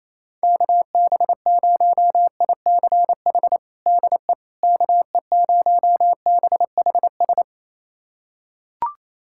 Morse code, heard as a keyed tone, about 28 wpm, 700 Hz. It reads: KB0ICH DE KE0BHS
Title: USA Callsign DE Callsign - 6 Characters - 28wpm